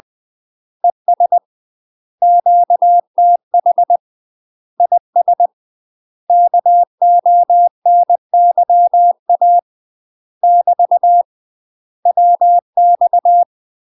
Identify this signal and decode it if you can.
Morse code, heard as a keyed tone, about 20 wpm, 700 Hz. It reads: ES QTH IS KONYA <BT> WX
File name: UK MASTER QSOs TRAINING / 20WPM / UK MASTER QSOs 0062